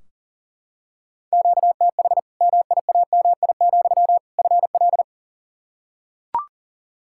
Morse code, heard as a keyed tone, about 37 wpm, 700 Hz. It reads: QTH MIAMI, FL
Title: Practice QSO - Exchange 1 - Beginning - 40wpm